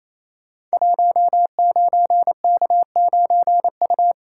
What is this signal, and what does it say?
Morse code, heard as a keyed tone, about 28 wpm, 700 Hz. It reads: 19K9U
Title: Code Groups - 5 Letter:Numbers - 28wpm